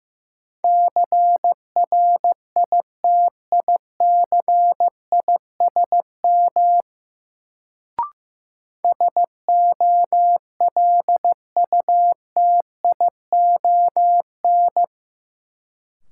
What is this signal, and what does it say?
Morse code, heard as a keyed tone, about 16 wpm, 700 Hz. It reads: CRITICISM SOLUTION
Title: Sentences from Top 1300 Words - New Word Drill - 15wpm